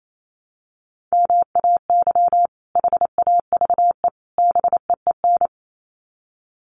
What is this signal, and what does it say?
Morse code, heard as a keyed tone, about 28 wpm, 700 Hz. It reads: MAY HAVE BEEN
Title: Sets of 3 Words 28wpm